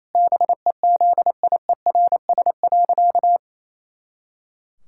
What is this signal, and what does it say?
Morse code, heard as a keyed tone, about 28 wpm, 700 Hz. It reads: BEZIERS.
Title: Pangram Challenge v2 - 28wpm